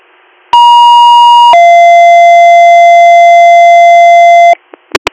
Two-tone page, heard as a one-second tone, then a three-second tone, about 950 then 690 Hz. It is heard from 0.5 to 4.5 s.